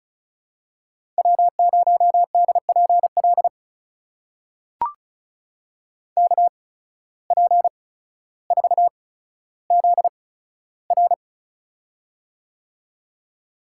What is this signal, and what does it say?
Morse code, heard as a keyed tone, about 34 wpm, spaced as for 15 wpm, 700 Hz. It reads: W0DPL KP4ZR
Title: USA Call Signs - 5 Characters - 8x Character Spacing - 35wpm